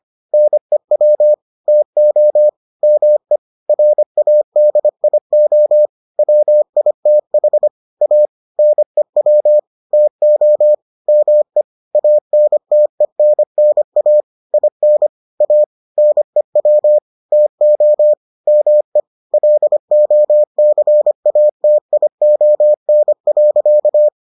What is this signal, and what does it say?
Morse code, heard as a keyed tone, about 25 wpm, 600 Hz. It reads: NEW TO ME RADIO WITH A NEW TO ME ANTENNA IN A NEW TO ME LOCATION.